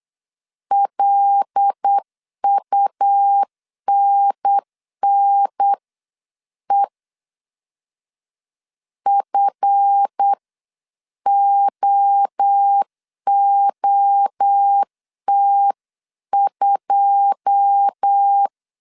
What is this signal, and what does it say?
Morse code, heard as a keyed tone, about 8 wpm, spaced as for 7 wpm, 800 Hz. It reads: LUNNE FOOT2
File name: CW / DET / 1 9 1500C S A 8wpm 5637